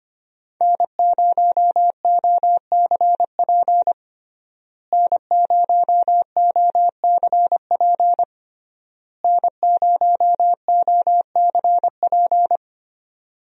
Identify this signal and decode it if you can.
Morse code, heard as a keyed tone, about 25 wpm, 700 Hz. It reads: N0OCP N0OCP N0OCP
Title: USA Call Signs - 5 Characters - Repeated 3x - 25wpm